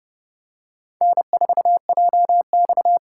Morse code, heard as a keyed tone, about 28 wpm, 700 Hz. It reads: N4JX